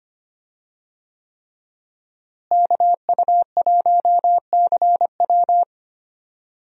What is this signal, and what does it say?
Morse code, heard as a keyed tone, about 25 wpm, 700 Hz. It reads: KU1CW
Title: USA Call Signs - 5 Characters - Repeated 3x - 25wpm